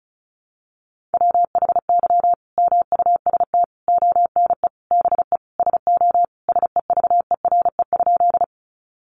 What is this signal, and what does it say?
Morse code, heard as a keyed tone, about 34 wpm, 700 Hz. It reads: WHY MUST ONE BE SO SEVERE?
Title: Sentences from Top 1400 Words 35wpm